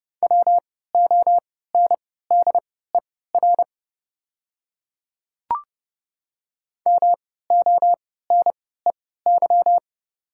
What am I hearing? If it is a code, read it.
Morse code, heard as a keyed tone, about 30 wpm, spaced as for 17 wpm, 690 Hz. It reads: WONDER MONEY
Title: 3x Character Spacing - Top 500 Words - 30wpm